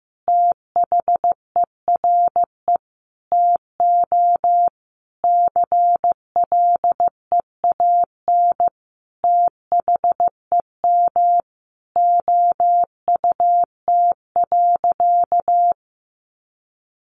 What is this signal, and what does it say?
Morse code, heard as a keyed tone, about 15 wpm, 700 Hz. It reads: THERE TO CLEAN THEM OUT.
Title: Sentences from Top 1500 Words 15wpm